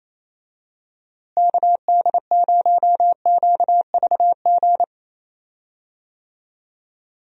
Morse code, heard as a keyed tone, about 28 wpm, 700 Hz. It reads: KD0QVG
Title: USA Call Signs - 6 Characters - Repeated 3x - 28wpm